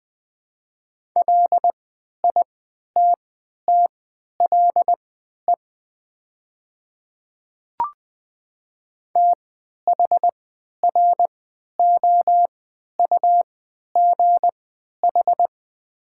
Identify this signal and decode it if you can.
Morse code, heard as a keyed tone, about 21 wpm, spaced as for 11 wpm, 700 Hz. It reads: LITTLE THROUGH
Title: 3x Character Spacing - Top 500 Words - 20wpm